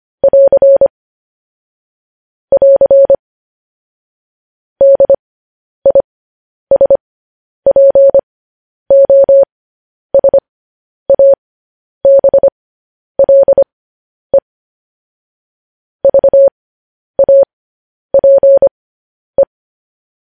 Morse code, heard as a keyed tone, about 25 wpm, spaced as for 10 wpm, 550 Hz. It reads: <AR> <AR> DISPOSABLE VAPE
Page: news-headlines-in-morse-code-at-10-wpm